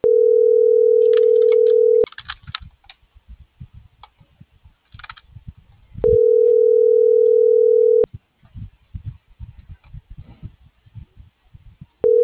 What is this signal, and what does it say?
Ringback tone (North America), 440 + 480 Hz.